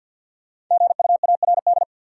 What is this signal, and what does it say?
Morse code, heard as a keyed tone, about 52 wpm, 700 Hz. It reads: GUARD
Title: Sentences from Top 700 Words - New Word Drill - 50wpm